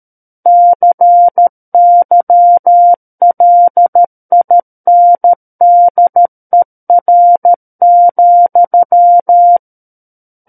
Morse code, heard as a keyed tone, about 13 wpm, 700 Hz. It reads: CYLINDER,
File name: Book One / 04-Chapter 4 - 13 WPM